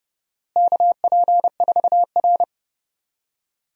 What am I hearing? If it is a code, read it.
Morse code, heard as a keyed tone, about 30 wpm, 700 Hz. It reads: KP4R
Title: DXCC-100 Call Signs - 4 Characters - Repeated 3x - 30wpm